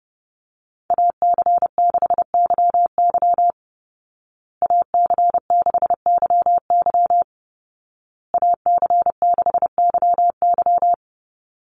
Morse code, heard as a keyed tone, about 30 wpm, 700 Hz. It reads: AC6YY AC6YY AC6YY
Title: Call Signs Repeated 2x - 30wpm